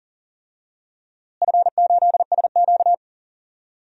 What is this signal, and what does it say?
Morse code, heard as a keyed tone, about 40 wpm, 700 Hz. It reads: F8SQ
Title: DXCC-100 Call Signs - 4 Characters - Repeated 2x - 40wpm